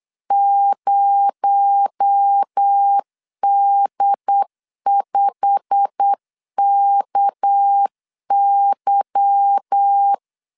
Morse code, heard as a keyed tone, about 8 wpm, 800 Hz. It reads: 0D5KY
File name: LEZIONI / 8 YJ45 / 8 YJ45 8wpm 3214